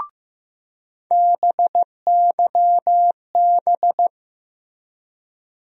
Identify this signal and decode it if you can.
Morse code, heard as a keyed tone, about 15 wpm, 700 Hz. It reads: BYB
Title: USA Call Sign Postfixes - 3 Letters 15wpm